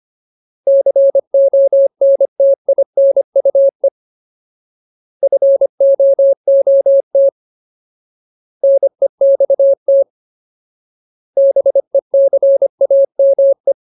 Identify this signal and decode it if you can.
Morse code, heard as a keyed tone, about 25 wpm, 550 Hz. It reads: CONTINUE FOOT NEXT BECAME
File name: Ultimate Word List / 25WPM Ultimate